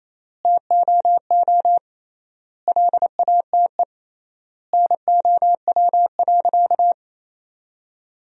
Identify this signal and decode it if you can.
Morse code, heard as a keyed tone, about 28 wpm, 700 Hz. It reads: TOO LATE NOW.
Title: 3x Word-Spacing - Sentences from Top 300 Words - 28wpm